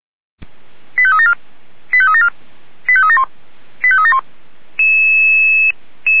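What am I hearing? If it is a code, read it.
Five-tone selective call (ZVEI), heard as quick runs of five tones, tones of about 70 ms. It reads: 85374 85374 85371 85371
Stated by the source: ZVEI five-tone selective call received on SDR.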